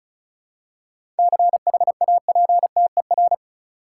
Morse code, heard as a keyed tone, about 36 wpm, 700 Hz. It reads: CHAPTER
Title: Seven Letter Words - 35wpm